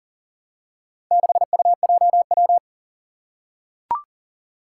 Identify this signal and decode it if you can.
Morse code, heard as a keyed tone, about 42 wpm, 700 Hz. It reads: 6UJW
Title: Four Challenging Letter-Number Combinations 40wpm